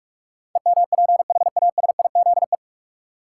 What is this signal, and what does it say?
Morse code, heard as a keyed tone, about 46 wpm, 700 Hz. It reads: EMPHASIZE
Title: Sentences from Top 1500 Words - New Word Drill - 45wpm